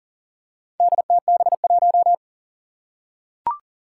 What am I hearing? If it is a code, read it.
Morse code, heard as a keyed tone, about 40 wpm, 700 Hz. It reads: DTB1
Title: Four Challenging Letter-Number Combinations 40wpm